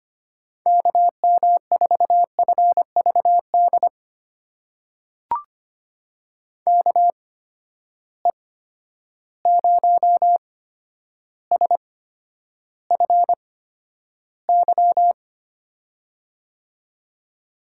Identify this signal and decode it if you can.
Morse code, heard as a keyed tone, about 25 wpm, spaced as for 11 wpm, 700 Hz. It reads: KM4FVD KE0SFY
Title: USA Call Signs - 6 Characters - 8x Character Spacing - 25wpm